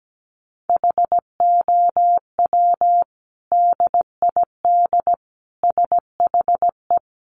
Morse code, heard as a keyed tone, about 17 wpm, 700 Hz. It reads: HOW DID SHE
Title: Sentences from Top 800 Words 17wpm